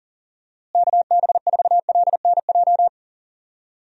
Morse code, heard as a keyed tone, about 40 wpm, 700 Hz. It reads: KB4LNJ
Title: USA Call Signs - 6 Characters - Repeated 3x - 40wpm